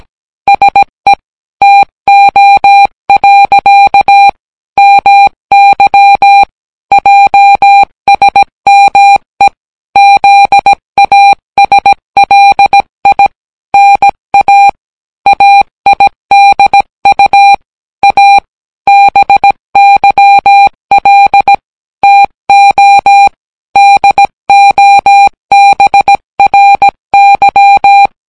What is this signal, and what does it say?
Morse code, heard as a keyed tone, about 17 wpm, 800 Hz. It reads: SE TO. MY JSME ZASLI NA AIDU A BYL TO DOBRY